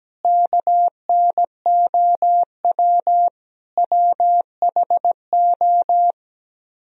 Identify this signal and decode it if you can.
Morse code, heard as a keyed tone, about 17 wpm, 700 Hz. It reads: KNOW WHO
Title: Speed-Racing - Sets of 5 Words 17wpm